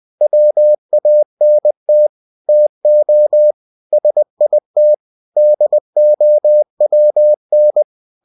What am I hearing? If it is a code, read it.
Morse code, heard as a keyed tone, about 20 wpm, 600 Hz. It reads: WANT TO SIT DOWN
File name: cw-books / winnie-the-pooh / winnie-20wpm-page-0079